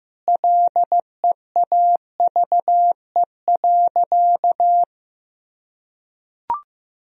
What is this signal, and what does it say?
Morse code, heard as a keyed tone, about 15 wpm, 700 Hz. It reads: LEAVE.
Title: Sentences from Top 1000 Words - Review Part 4 - 15wpm